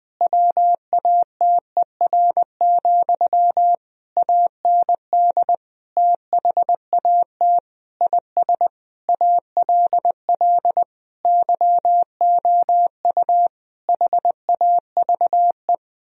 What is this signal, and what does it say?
Morse code, heard as a keyed tone, about 20 wpm, 700 Hz. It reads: WATER, AND THAT IS ALL YOU HAVE
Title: Sentences from Top 1000 Words - Review Part 2 - 20wpm